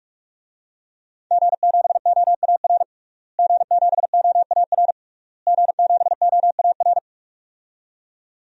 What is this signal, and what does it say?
Morse code, heard as a keyed tone, about 41 wpm, 700 Hz. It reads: G7OAR G7OAR G7OAR
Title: DXCC-100 Call Signs - 5 Characters - Repeated 3x - 45wpm